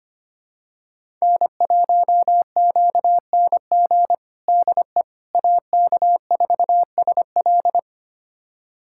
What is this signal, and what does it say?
Morse code, heard as a keyed tone, about 25 wpm, 700 Hz. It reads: N1QNG DE AK4SL
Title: USA Callsign DE Callsign - 5 Characters - 25wpm